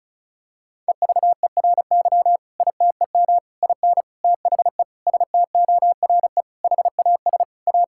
Morse code, heard as a keyed tone, about 36 wpm, 700 Hz. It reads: EVERY ITEM IN THE STORE HAS A
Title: Sentences from Top 1000 Words - Review Part 6 - 35wpm